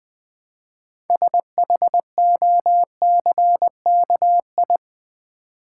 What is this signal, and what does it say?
Morse code, heard as a keyed tone, about 20 wpm, 700 Hz. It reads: SHOCKI
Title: Build A Word - 3 Letters At A Time - 8-Letter Words - 20wpm